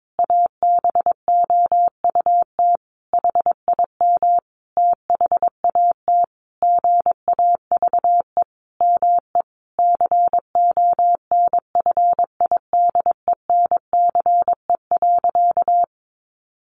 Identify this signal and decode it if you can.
Morse code, heard as a keyed tone, about 22 wpm, 700 Hz. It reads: ABOUT HIM THAT GAVE ME CONFIDENCE.
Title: Sentences from Top 1500 Words 22wpm